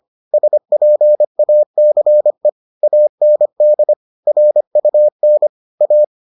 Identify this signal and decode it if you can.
Morse code, heard as a keyed tone, about 25 wpm, 600 Hz. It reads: SPACE AND RUN A